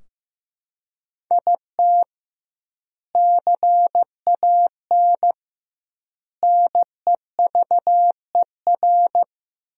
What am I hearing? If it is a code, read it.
Morse code, heard as a keyed tone, about 15 wpm, 700 Hz. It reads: IT CAN NEVER
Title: 2x Word-Spacing - Sentences from Top 400 Words - 15wpm